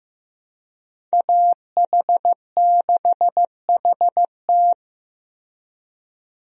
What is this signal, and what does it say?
Morse code, heard as a keyed tone, about 15 wpm, 700 Hz. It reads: AH6HT
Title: USA Call Signs - 5 Characters - Repeated 3x - 15wpm